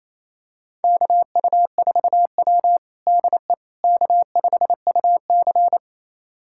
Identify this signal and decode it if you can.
Morse code, heard as a keyed tone, about 28 wpm, 700 Hz. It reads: KU4W DE K5UC
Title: USA Callsign DE Callsign - 4 Characters - 28wpm